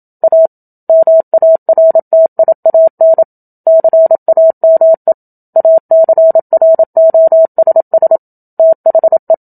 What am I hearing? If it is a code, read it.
Morse code, heard as a keyed tone, about 27 wpm, 650 Hz. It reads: A MARTIAN CAME ACROSS THE
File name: Book Two / 01-Chapter 1 - 27 WPM